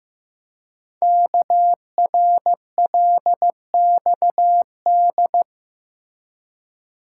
Morse code, heard as a keyed tone, about 15 wpm, 700 Hz. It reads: KRLXD
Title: Code Groups - 5 Letters - 15wpm